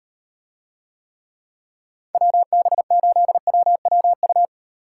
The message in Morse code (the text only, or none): WB8WWU